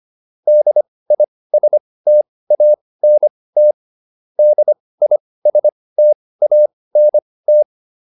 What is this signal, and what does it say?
Morse code, heard as a keyed tone, about 25 wpm, spaced as for 18 wpm, 600 Hz. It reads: DISTANT DISTANT